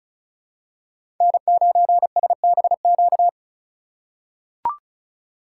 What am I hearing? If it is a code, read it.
Morse code, heard as a keyed tone, about 36 wpm, 700 Hz. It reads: N9SBQ
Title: USA Call Signs - 4x Character Spacing - 35wpm